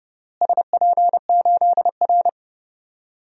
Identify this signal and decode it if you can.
Morse code, heard as a keyed tone, about 30 wpm, 700 Hz. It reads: SP8R
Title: DXCC-100 Call Signs - 4 Characters - Repeated 3x - 30wpm